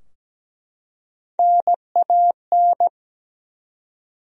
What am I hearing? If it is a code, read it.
Morse code, heard as a keyed tone, about 17 wpm, 700 Hz. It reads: NAN
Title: USA Common Names - Max 3 Letters - 17wpm